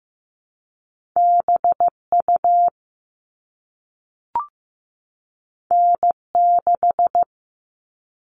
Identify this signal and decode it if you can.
Morse code, heard as a keyed tone, about 15 wpm, 700 Hz. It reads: BU N6
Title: All Cut Numbers - 2-Digit 15wpm